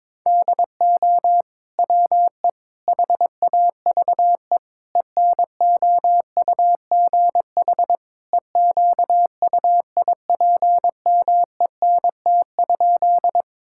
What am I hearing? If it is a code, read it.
Morse code, heard as a keyed tone, about 21 wpm, 700 Hz. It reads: DO WE HAVE ENOUGH EQUIPMENT?
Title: Sentences from Top 1000 Words - Review Part 5 - 22wpm